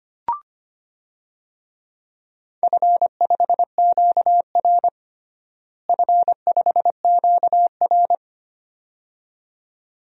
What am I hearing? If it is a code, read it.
Morse code, heard as a keyed tone, about 25 wpm, 700 Hz. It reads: F5QR F5QR
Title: DXCC-100 Call Signs - 4 Characters - Repeated 2x - 25wpm